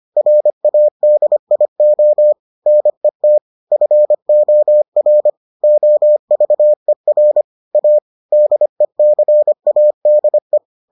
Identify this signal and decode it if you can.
Morse code, heard as a keyed tone, about 25 wpm, 600 Hz. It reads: RADIO NET FOR OVER A DECADE